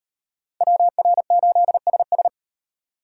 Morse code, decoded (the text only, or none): WR8SS